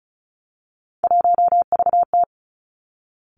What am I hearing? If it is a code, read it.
Morse code, heard as a keyed tone, about 36 wpm, 700 Hz. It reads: 1VT
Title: Three Challenging Letter-Number Combinations 35wpm